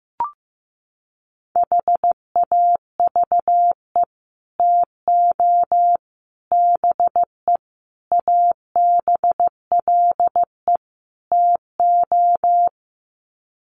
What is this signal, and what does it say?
Morse code, heard as a keyed tone, about 15 wpm, 700 Hz. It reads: HAVE TO BE ABLE TO
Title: Sets of 5 Words 15wpm